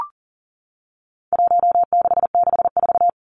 Morse code, heard as a keyed tone, about 40 wpm, 700 Hz. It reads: 1664
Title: Numbers - Contest Style - 4-Digit 40wpm